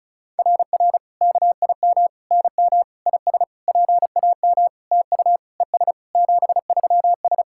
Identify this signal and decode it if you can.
Morse code, heard as a keyed tone, about 36 wpm, 690 Hz. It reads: RR KIM NM IS PAM TU ES 73S
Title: SOTA QSO Practice 35wpm